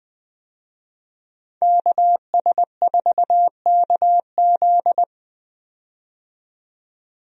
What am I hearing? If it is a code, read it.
Morse code, heard as a keyed tone, about 20 wpm, 700 Hz. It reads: KS4KZ
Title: USA Call Signs - 5 Characters - 20wpm